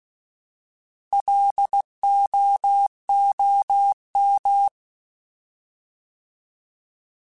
Morse code, heard as a keyed tone, about 16 wpm, 800 Hz. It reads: LOOM
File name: Alphabetical / Alphabetical mp3s / Ls @15